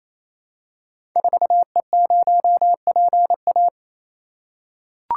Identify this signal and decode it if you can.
Morse code, heard as a keyed tone, about 28 wpm, 700 Hz. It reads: 4E0PA
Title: Code Groups - 5 Letter:Numbers - 28wpm